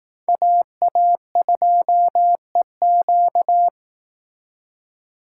Morse code, heard as a keyed tone, about 18 wpm, 700 Hz. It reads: AA2EQ
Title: Speed Racing - USA 2x2 Call Signs 15wpm